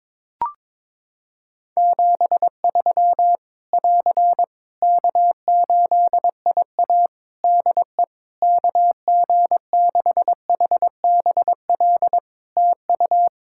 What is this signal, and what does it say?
Morse code, heard as a keyed tone, about 22 wpm, 700 Hz. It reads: 73 <AR> K8IA DE KG6HBL TU
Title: Practice QSO - Exchange 4 - Ending - 22wpm